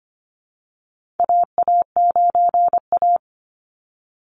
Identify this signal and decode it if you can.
Morse code, heard as a keyed tone, about 25 wpm, 700 Hz. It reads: AA9A
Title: Call Signs - 25wpm